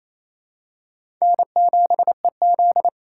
Morse code, heard as a keyed tone, about 28 wpm, 700 Hz. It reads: N7EZ
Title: USA Call Signs - 4 Characters - 2x Character Spacing - 28wpm